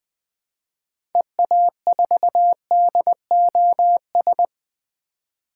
Morse code, heard as a keyed tone, about 21 wpm, 700 Hz. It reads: EA4DOS
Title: 2023 SOTA Activator Call Signs - 20wpm